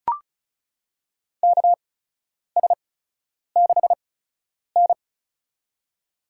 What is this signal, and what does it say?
Morse code, heard as a keyed tone, about 34 wpm, spaced as for 10 wpm, 700 Hz. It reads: KS6N
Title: USA Call Signs - 4 Characters - 8x Character Spacing - 35wpm